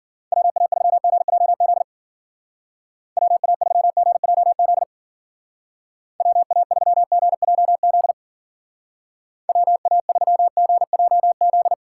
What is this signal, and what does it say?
Morse code, heard as a keyed tone, about 49 wpm, 700 Hz. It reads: WA3GJZ WA3GJZ WA3GJZ WA3GJZ
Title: Speed Racing - USA Call Signs 40wpm